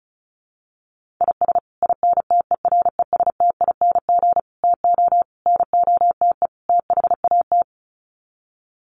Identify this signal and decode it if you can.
Morse code, heard as a keyed tone, about 34 wpm, 700 Hz. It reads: IS INTERESTING TO NOTE THAT
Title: Sets of 5 Words Encore2 35wpm